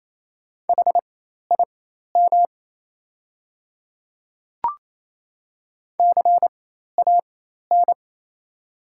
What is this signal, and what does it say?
Morse code, heard as a keyed tone, about 28 wpm, spaced as for 13 wpm, 700 Hz. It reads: HIM CAN
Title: 4x Character Spacing - Top 100 Words - 28wpm